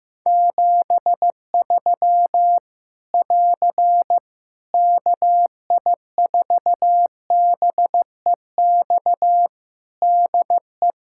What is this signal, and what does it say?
Morse code, heard as a keyed tone, about 15 wpm, 700 Hz. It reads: 73 <AR> KI4BEX DE
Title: Practice QSO - Exchange 4 - Ending - 15wpm